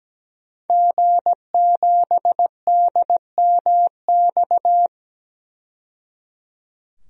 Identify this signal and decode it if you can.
Morse code, heard as a keyed tone, about 17 wpm, 700 Hz. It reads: G7DMX
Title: DXCC-100 Call Signs - 5 Characters - 17wpm